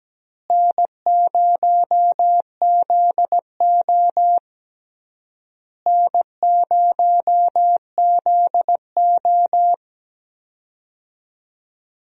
Morse code, heard as a keyed tone, about 17 wpm, 700 Hz. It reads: N0ZO N0ZO
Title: USA Call Signs - 4 Characters - Repeated 3x - 17wpm